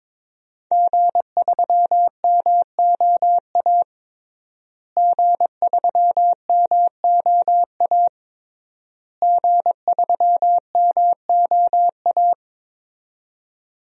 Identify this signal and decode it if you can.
Morse code, heard as a keyed tone, about 22 wpm, 700 Hz. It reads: G3MOA G3MOA G3MOA
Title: DXCC-100 Call Signs - 5 Characters - Repeated 3x - 22wpm